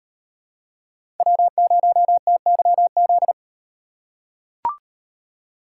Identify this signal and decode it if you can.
Morse code, heard as a keyed tone, about 38 wpm, 700 Hz. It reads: W0TYZ